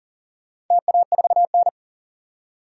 Morse code, heard as a keyed tone, about 40 wpm, 700 Hz. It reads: TA4N
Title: DXCC-100 Call Signs - 4 Characters - 40wpm